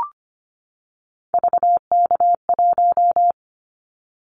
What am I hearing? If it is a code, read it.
Morse code, heard as a keyed tone, about 25 wpm, 700 Hz. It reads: VK1
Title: Callsign Prefixes for Top 50 DXCC - 25wpm